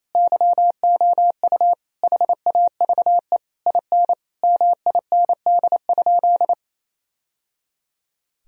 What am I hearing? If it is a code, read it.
Morse code, heard as a keyed tone, about 28 wpm, 700 Hz. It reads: YOU HAVE IN MIND?
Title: Sentences from Top 1000 Words - Review Part 3 - 28wpm